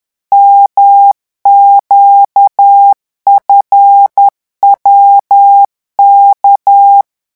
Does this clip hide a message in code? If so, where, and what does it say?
Yes: Morse code from 0.3 to 7.0 s, heard as a keyed tone, about 10 wpm, 800 Hz. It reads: MQFWK